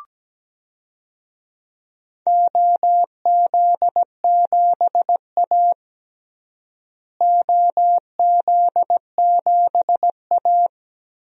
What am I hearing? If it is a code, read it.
Morse code, heard as a keyed tone, about 17 wpm, 700 Hz. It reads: OZ7A OZ7A
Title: DXCC-100 Call Signs - 4 Characters - Repeated 3x - 17wpm